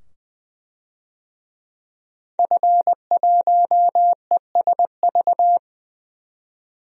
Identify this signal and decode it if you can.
Morse code, heard as a keyed tone, about 20 wpm, 700 Hz. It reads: F1ESV
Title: DXCC-100 Call Signs - 5 Characters - 20wpm